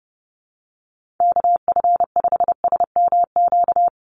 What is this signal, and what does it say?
Morse code, heard as a keyed tone, about 30 wpm, 700 Hz. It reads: KF5SMQ
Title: USA Call Signs Repeated 3x 30wpm